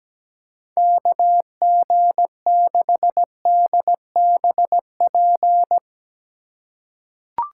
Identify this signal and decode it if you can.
Morse code, heard as a keyed tone, about 17 wpm, 700 Hz. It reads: KG6DBP
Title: USA Call Signs - 2x Character Spacing - 17wpm